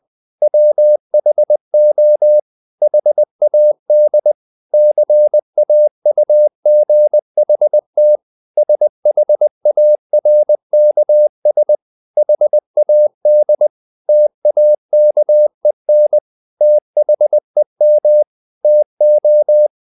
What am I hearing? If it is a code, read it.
Morse code, heard as a keyed tone, about 20 wpm, 600 Hz. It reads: WHO HAD CAUGHT SHARKS HAD TAKEN THEM TO
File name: the old man and the sea / the old man and the sea part 1 20wpm